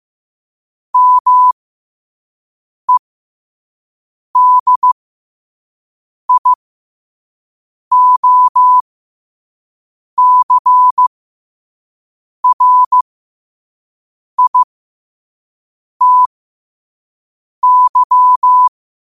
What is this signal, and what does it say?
Morse code, heard as a keyed tone, about 15 wpm, spaced as for 5 wpm, 1000 Hz. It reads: MEDIOCRITY
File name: qotdfiles / 5wpm